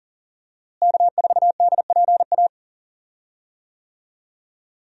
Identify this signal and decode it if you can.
Morse code, heard as a keyed tone, about 37 wpm, 700 Hz. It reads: K4DPA
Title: USA Call Signs - 5 Characters - Repeated 2x - 40wpm